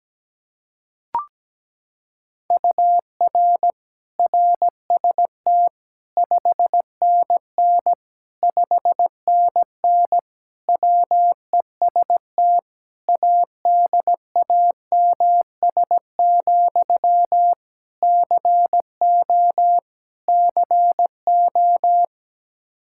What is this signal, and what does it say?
Morse code, heard as a keyed tone, about 17 wpm, 700 Hz. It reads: UR RST 5NN 5NN WEST ADAMS, CO CO
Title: K1USN SST QSO Practice 17wpm